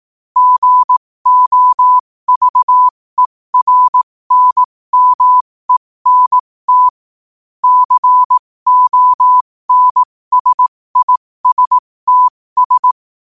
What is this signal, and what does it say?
Morse code, heard as a keyed tone, about 18 wpm, spaced as for 15 wpm, 1000 Hz. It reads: GOVERNMENT CONSISTS
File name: qotdfiles / 15wpm